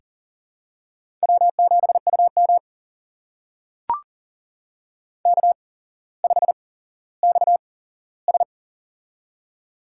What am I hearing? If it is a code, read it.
Morse code, heard as a keyed tone, about 40 wpm, spaced as for 17 wpm, 700 Hz. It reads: W7UM K5XS
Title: USA Call Signs - 4 Characters - 8x Character Spacing - 40wpm